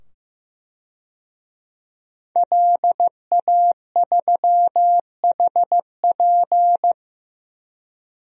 Morse code, heard as a keyed tone, about 15 wpm, 700 Hz. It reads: LA3HP
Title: DXCC-100 Call Signs - 5 Characters - 15wpm